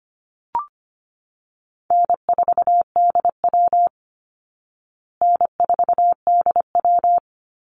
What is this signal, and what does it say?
Morse code, heard as a keyed tone, about 25 wpm, 700 Hz. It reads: N4DW N4DW
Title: Easy Call Signs Repeated 2x - 25wpm